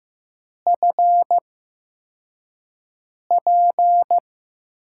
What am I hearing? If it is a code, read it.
Morse code, heard as a keyed tone, about 15 wpm, spaced as for 4 wpm, 700 Hz. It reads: FP
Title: USA Call Signs - 6 Characters - 8x Character Spacing - 15wpm